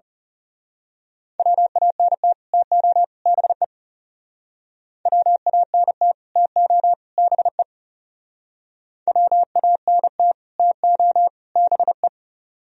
Morse code, heard as a keyed tone, about 34 wpm, 700 Hz. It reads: WANT TO BE WANT TO BE WANT TO BE
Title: Speed-Racing - Sets of 3 Words 30wpm